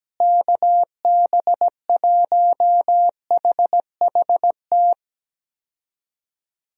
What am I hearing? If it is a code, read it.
Morse code, heard as a keyed tone, about 17 wpm, 700 Hz. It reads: KB1HHT
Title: USA Call Signs - 6 Characters - Repeated 3x - 17wpm